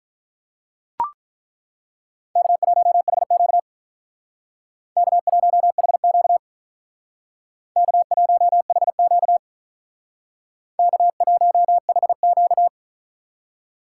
Morse code, heard as a keyed tone, about 43 wpm, 700 Hz. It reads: K1HQ K1HQ K1HQ K1HQ
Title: Speed Racing - USA 1x2 Call Signs 35wpm